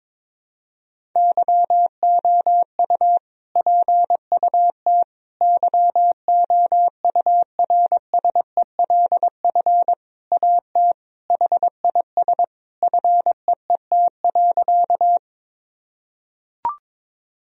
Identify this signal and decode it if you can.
Morse code, heard as a keyed tone, about 22 wpm, 700 Hz. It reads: YOU PUT YOURSELF AT HIS FEET.
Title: Sentences from Top 1000 Words - Review Part 1 - 22wpm